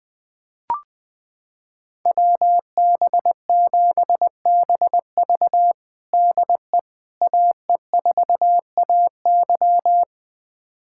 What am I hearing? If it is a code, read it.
Morse code, heard as a keyed tone, about 19 wpm, 700 Hz. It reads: WB7BV DE AE4AY
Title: USA Callsign DE Callsign - 5 Characters - 20wpm